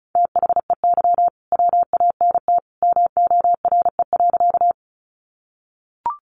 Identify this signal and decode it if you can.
Morse code, heard as a keyed tone, about 34 wpm, 700 Hz. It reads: THEY WANT MORE.
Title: Sentences from Top 200 Words 35wpm